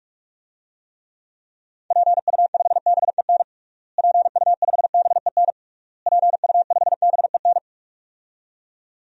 Morse code, heard as a keyed tone, about 41 wpm, 700 Hz. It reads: PU5BEN PU5BEN PU5BEN
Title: DXCC-100 Call Signs - 6 Characters - Repeated 3x - 45wpm